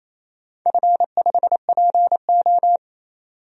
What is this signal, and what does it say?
Morse code, heard as a keyed tone, about 28 wpm, 700 Hz. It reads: F5PO
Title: DXCC-100 Call Signs - 4 Characters - 28wpm